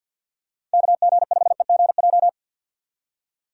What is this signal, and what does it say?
Morse code, heard as a keyed tone, about 50 wpm, 700 Hz. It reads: KG5EGJ
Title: USA Call Signs - 4x Character Spacing - 50wpm